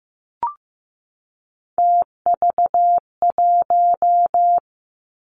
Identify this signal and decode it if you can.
Morse code, heard as a keyed tone, about 15 wpm, 700 Hz. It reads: TV1
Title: Three Challenging Letter-Number Combinations 15wpm